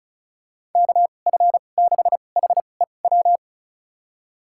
Morse code, heard as a keyed tone, about 36 wpm, spaced as for 25 wpm, 700 Hz. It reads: KF6HEW
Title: USA Call Signs - 6 Characters - 2x Character Spacing - 35wpm